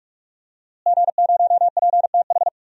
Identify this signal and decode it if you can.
Morse code, heard as a keyed tone, about 41 wpm, 700 Hz. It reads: G0PTH
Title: DXCC-100 Call Signs - 5 Characters - Repeated 2x - 45wpm